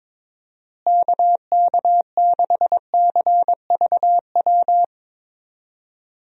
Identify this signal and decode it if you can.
Morse code, heard as a keyed tone, about 21 wpm, 700 Hz. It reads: KK6CVW
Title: USA Call Signs - 6 Characters - Repeated 3x - 22wpm